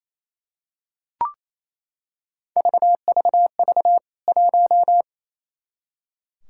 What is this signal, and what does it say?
Morse code, heard as a keyed tone, about 28 wpm, 700 Hz. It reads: VVV 1
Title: K1USN SST QSO Practice 28wpm